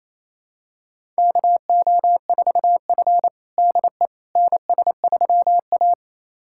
Morse code, heard as a keyed tone, about 27 wpm, 700 Hz. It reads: KO4F DE NS3A
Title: USA Callsign DE Callsign - 4 Characters - 28wpm